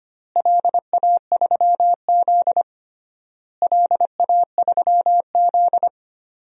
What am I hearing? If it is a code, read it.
Morse code, heard as a keyed tone, about 25 wpm, 700 Hz. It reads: LA3Z LA3Z
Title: DXCC-100 Call Signs - 4 Characters - Repeated 3x - 25wpm